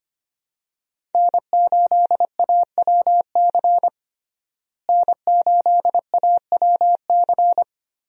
Morse code, heard as a keyed tone, about 25 wpm, 700 Hz. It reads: N8AWC N8AWC
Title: USA Call Signs - 5 Characters - Repeated 2x - 25wpm